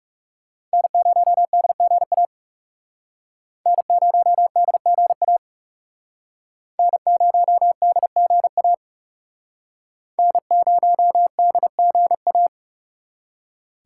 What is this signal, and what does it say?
Morse code, heard as a keyed tone, about 37 wpm, 700 Hz. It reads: N0DGA N0DGA N0DGA N0DGA
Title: Speed Racing - USA Call Signs 30wpm